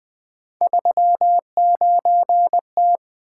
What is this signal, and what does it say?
Morse code, heard as a keyed tone, about 20 wpm, 700 Hz. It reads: 39T
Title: Three Letter-Number Combinations - 20wpm